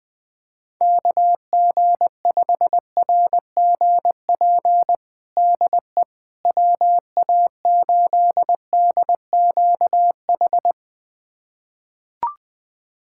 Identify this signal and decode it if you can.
Morse code, heard as a keyed tone, about 21 wpm, 700 Hz. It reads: KG5RGP DE WA8DQH
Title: USA Callsign DE Callsign 20wpm